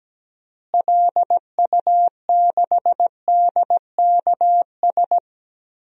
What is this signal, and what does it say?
Morse code, heard as a keyed tone, about 17 wpm, 700 Hz. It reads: LU6DKS
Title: DXCC-100 Call Signs - 6 Characters - Repeated 3x - 17wpm